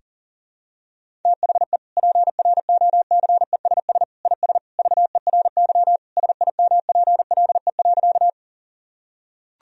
Morse code, heard as a keyed tone, about 40 wpm, 700 Hz. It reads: THE PROCESS IS VERY SIMPLE.
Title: Sentences from Top 1000 Words - Review Part 3 - 40wpm